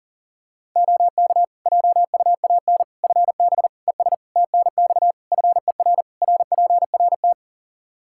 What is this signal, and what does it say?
Morse code, heard as a keyed tone, about 37 wpm, 700 Hz. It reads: OK JUAN FB ES TNX FER RPRT
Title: K1USN SST QSO Practice 40wpm